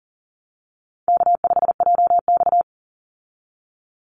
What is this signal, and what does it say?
Morse code, heard as a keyed tone, about 40 wpm, 700 Hz. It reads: K5JX
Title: Call Signs Repeated 2x - 40wpm